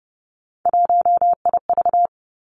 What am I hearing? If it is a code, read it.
Morse code, heard as a keyed tone, about 30 wpm, 700 Hz. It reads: 1IV
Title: Three Challenging Letter-Number Combinations 30wpm